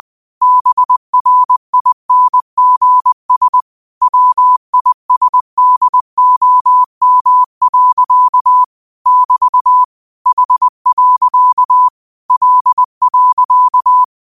Morse code, heard as a keyed tone, about 20 wpm, 1000 Hz. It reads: BRINGS WISDOM. <BT> H. L.
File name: qotdfiles / 20wpm